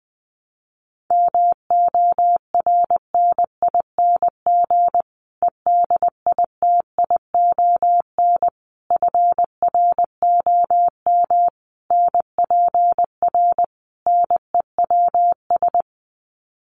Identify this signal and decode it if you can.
Morse code, heard as a keyed tone, about 19 wpm, 700 Hz. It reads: MORNING EDITION FROM NPR NEWS
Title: Sets of 5 Words Encore 20wpm